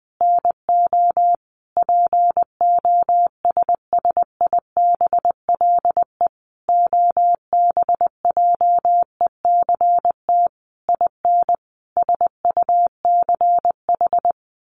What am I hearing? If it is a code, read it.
Morse code, heard as a keyed tone, about 19 wpm, 700 Hz. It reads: NO POSSIBLE OBJECT IN SUCH
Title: Sentences from Top 1500 Words 20wpm